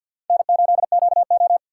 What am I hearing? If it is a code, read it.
Morse code, heard as a keyed tone, about 51 wpm, 700 Hz. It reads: N8QO
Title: USA Call Signs - 4 Characters - 3x Character Spacing - 50wpm